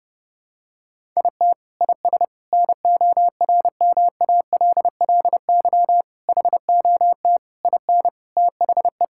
Morse code, heard as a keyed tone, about 32 wpm, 700 Hz. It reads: IT IS NORMALLY HOT IN THE
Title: Sentences from Top 1000 Words - Review Part 6 - 30wpm